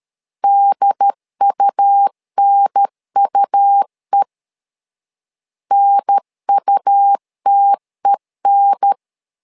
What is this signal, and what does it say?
Morse code, heard as a keyed tone, about 13 wpm, 800 Hz. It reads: DUNUE NUTEN
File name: LEZIONI / 2 UHDN / 2 UHDN 12wpm 1456